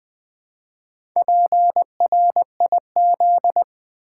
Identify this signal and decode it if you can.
Morse code, heard as a keyed tone, about 21 wpm, 700 Hz. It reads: PRIZ
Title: Build A Word - 1 Letter At A Time - 5-Letter Words - 20wpm